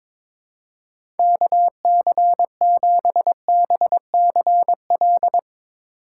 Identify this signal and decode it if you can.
Morse code, heard as a keyed tone, about 21 wpm, 700 Hz. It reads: KC7BCL
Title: USA Call Signs - 6 Characters - 3x Character Spacing - 22wpm